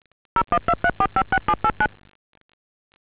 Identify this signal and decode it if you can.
DTMF, heard as telephone touch tones, tones of about 70 ms. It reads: 0123456789